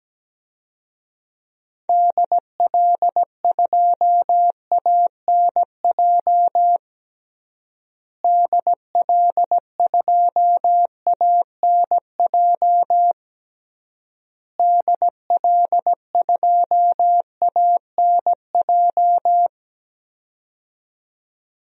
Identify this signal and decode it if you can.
Morse code, heard as a keyed tone, about 17 wpm, 700 Hz. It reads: DL2ANJ DL2ANJ DL2ANJ
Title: DXCC-100 Call Signs - 6 Characters - Repeated 3x - 17wpm